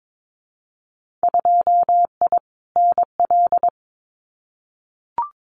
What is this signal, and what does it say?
Morse code, heard as a keyed tone, about 22 wpm, 700 Hz. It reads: 2I NL
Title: Winter Field Day Exchanges Encore 22wpm